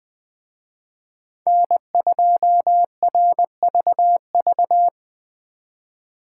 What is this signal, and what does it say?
Morse code, heard as a keyed tone, about 21 wpm, 700 Hz. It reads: N2RVV
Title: Speed Racing - USA 1x3 Call Signs 17wpm